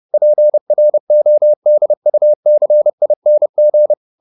Morse code, heard as a keyed tone, about 30 wpm, 600 Hz. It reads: PRODUCING